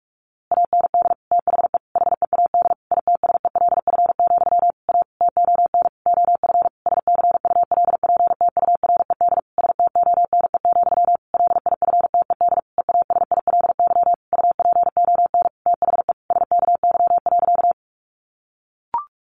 Morse code, heard as a keyed tone, about 44 wpm, 700 Hz. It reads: AND THE HEAD ITSELF, A TON OF SCULPTURED STONE, LIFTED EASILY UPON THE SKY.
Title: A Sound Of Thunder 45wpm